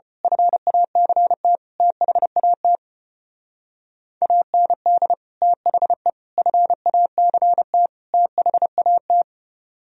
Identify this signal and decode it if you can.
Morse code, heard as a keyed tone, about 29 wpm, 700 Hz. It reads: FACT THAT AND THE FACT THAT
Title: Speed-Racing - Sets of 4 Words 25wpm